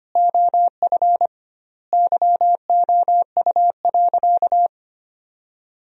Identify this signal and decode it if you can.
Morse code, heard as a keyed tone, about 25 wpm, 700 Hz. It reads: OF YOU.
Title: 2x Word-Spacing - Sentences from Top 200 Words - 25wpm